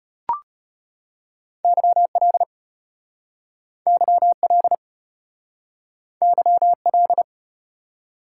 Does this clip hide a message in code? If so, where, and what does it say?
Yes: Morse code from 1.6 to 7.2 s, heard as a keyed tone, about 33 wpm, 700 Hz. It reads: YL YL YL